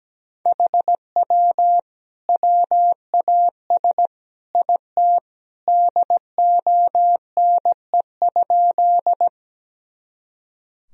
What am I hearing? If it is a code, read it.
Morse code, heard as a keyed tone, about 17 wpm, 700 Hz. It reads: HW WAS IT DONE?
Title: Sentences from Top 500 Words and Abbreviations 17wpm